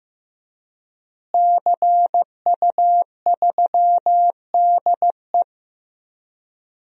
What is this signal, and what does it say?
Morse code, heard as a keyed tone, about 15 wpm, 700 Hz. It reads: CU3DE
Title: DXCC-100 Call Signs - 5 Characters - Repeated 3x - 15wpm